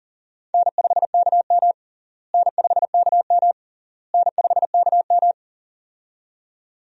Morse code, heard as a keyed tone, about 40 wpm, 700 Hz. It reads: N5KM N5KM N5KM
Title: USA Call Signs - 4 Characters - Repeated 3x - 40wpm